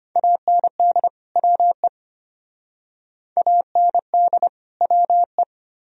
Morse code, heard as a keyed tone, about 32 wpm, 700 Hz. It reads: AND WE AND WE